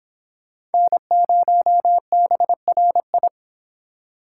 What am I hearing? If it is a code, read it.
Morse code, heard as a keyed tone, about 30 wpm, 700 Hz. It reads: N0BRI